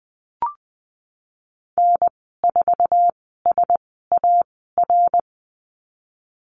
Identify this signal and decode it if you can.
Morse code, heard as a keyed tone, about 20 wpm, spaced as for 15 wpm, 700 Hz. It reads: N4SAR
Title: 069 Lesson 12g - Letter d - Call Signs 20wpm